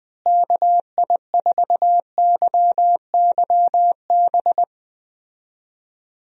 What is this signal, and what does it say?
Morse code, heard as a keyed tone, about 20 wpm, 700 Hz. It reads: KI4YYB
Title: USA Call Signs - 6 Characters - Repeated 3x - 20wpm